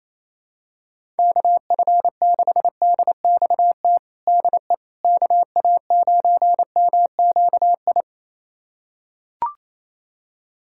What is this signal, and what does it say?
Morse code, heard as a keyed tone, about 28 wpm, 700 Hz. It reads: KF6DXT DE KA9MQI
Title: USA Callsign DE Callsign - 6 Characters - 28wpm